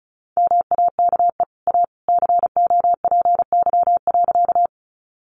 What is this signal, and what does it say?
Morse code, heard as a keyed tone, about 34 wpm, 700 Hz. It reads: MAKE A COPY.
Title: Sentences from Top 1400 Words 35wpm